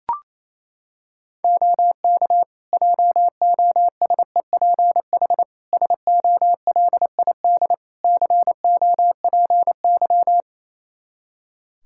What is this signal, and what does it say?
Morse code, heard as a keyed tone, about 28 wpm, 700 Hz. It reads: OK JOSEPH SOLID COPY
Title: Practice QSO - Exchange 3 - More Info - 28wpm